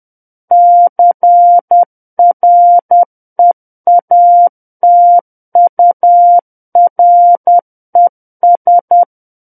Morse code, heard as a keyed tone, about 10 wpm, 700 Hz. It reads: CREATURES
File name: Book One / 01-Chapter 1 - 10 WPM